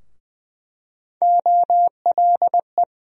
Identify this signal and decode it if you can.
Morse code, heard as a keyed tone, about 20 wpm, 700 Hz. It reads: OLE
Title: USA Call Sign Postfixes - 3 Letters 20wpm